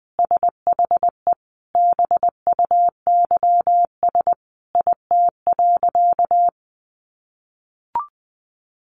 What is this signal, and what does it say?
Morse code, heard as a keyed tone, about 19 wpm, 700 Hz. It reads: SHE BUYS IT.
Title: Additional Easy Phrases 20wpm